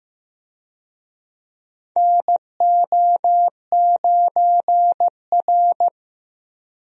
Morse code, heard as a keyed tone, about 15 wpm, 700 Hz. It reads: NO9R
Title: USA Call Signs - 4 Characters - Repeated 2x - 15wpm